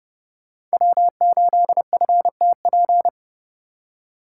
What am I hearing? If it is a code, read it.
Morse code, heard as a keyed tone, about 28 wpm, 700 Hz. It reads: W8FTP